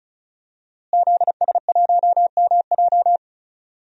Morse code, heard as a keyed tone, about 36 wpm, 700 Hz. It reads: ZS1MJ
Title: DXCC-100 Call Signs - 5 Characters - Repeated 2x - 35wpm